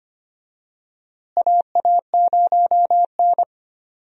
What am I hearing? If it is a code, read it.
Morse code, heard as a keyed tone, about 25 wpm, 700 Hz. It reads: AA0N
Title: USA Call Signs - 4 Characters - Repeated 2x - 25wpm